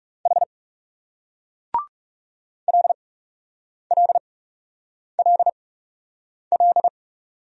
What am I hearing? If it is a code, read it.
Morse code, heard as a keyed tone, about 39 wpm, spaced as for 8 wpm, 700 Hz. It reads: H LLLL